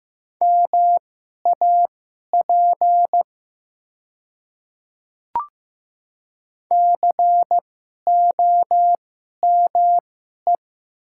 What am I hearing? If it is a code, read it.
Morse code, heard as a keyed tone, about 15 wpm, spaced as for 11 wpm, 700 Hz. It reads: MAP COME
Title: 2x Character Spacing - Top 500 Words - 15wpm